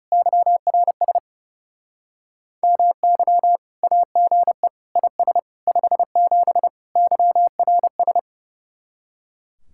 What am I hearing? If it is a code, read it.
Morse code, heard as a keyed tone, about 38 wpm, 700 Hz. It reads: YRS MY AGE IS 57 YRS